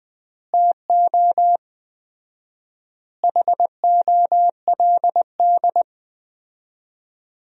Morse code, heard as a keyed tone, about 21 wpm, 700 Hz. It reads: TO HOLD
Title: 4x Word-Spacing - Sentences from Top 400 Words - 20wpm